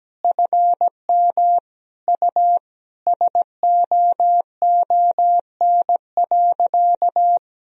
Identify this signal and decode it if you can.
Morse code, heard as a keyed tone, about 17 wpm, 700 Hz. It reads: FM U SOON.
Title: Sentences from Top 400 Words and Abbreviations 17wpm